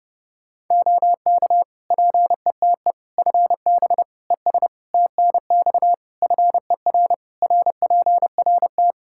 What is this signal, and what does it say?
Morse code, heard as a keyed tone, about 30 wpm, 700 Hz. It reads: OK PETE FB ES TNX FER RPRT
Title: Practice QSO - Exchange 2 - Info - 30wpm